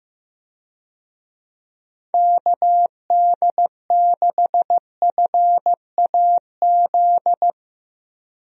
Morse code, heard as a keyed tone, about 15 wpm, 700 Hz. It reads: KD6FAZ
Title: USA Call Signs - 6 Characters - Repeated 2x - 15wpm